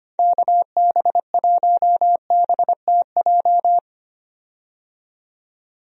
Morse code, heard as a keyed tone, about 25 wpm, 700 Hz. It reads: KB1BTJ
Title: USA Call Signs - 6 Characters - 25wpm